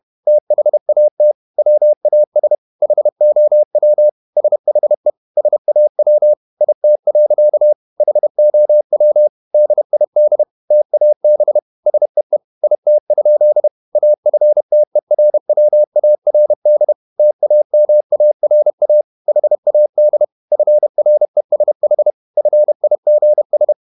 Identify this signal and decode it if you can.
Morse code, heard as a keyed tone, about 31 wpm, 600 Hz. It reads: THAT WAS HOW SHE SAW IT. HOW DID TAB SEE IT? AFTERWARD TAMARA HAD FRESH FIGS
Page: Never 31wpm | CW med Gnister